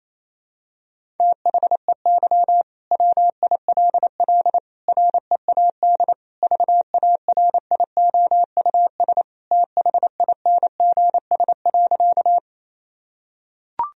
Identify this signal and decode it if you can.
Morse code, heard as a keyed tone, about 28 wpm, 700 Hz. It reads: THEY WILL READ VARIOUS THINGS.
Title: Sentences from Top 1000 Words - Review Part 4 - 28wpm